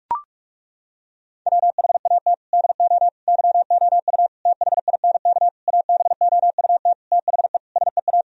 Morse code, heard as a keyed tone, about 41 wpm, 700 Hz. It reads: WHAT DO YOU THINK ABOUT THE SEA
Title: Sentences from Top 1000 Words - Review Part 4 - 45wpm